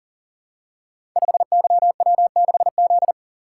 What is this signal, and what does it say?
Morse code, heard as a keyed tone, about 42 wpm, 690 Hz. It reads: 5YW6Z
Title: Code Groups - 5 Letter:Numbers - 40wpm